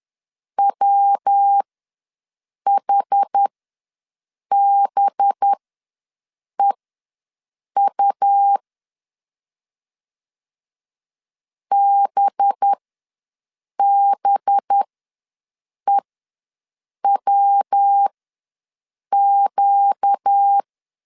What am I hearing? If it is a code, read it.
Morse code, heard as a keyed tone, about 10 wpm, spaced as for 6 wpm, 800 Hz. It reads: WHBEU BBEWQ
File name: LEZIONI / 5 LBVC / 5 LB ws 10wpm 3616